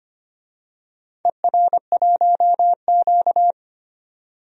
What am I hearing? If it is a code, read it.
Morse code, heard as a keyed tone, about 25 wpm, 690 Hz. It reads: ER1Q
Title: DXCC-100 Call Signs - 4 Characters - 25wpm